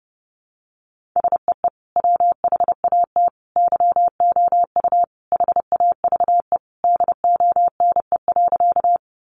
Morse code, heard as a keyed tone, about 30 wpm, 700 Hz. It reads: SEE WHAT YOU HAVE DONE.
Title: Sentences from Top 500 Words 30wpm